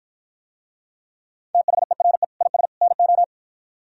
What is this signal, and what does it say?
Morse code, heard as a keyed tone, about 52 wpm, 700 Hz. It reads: THERE IS NO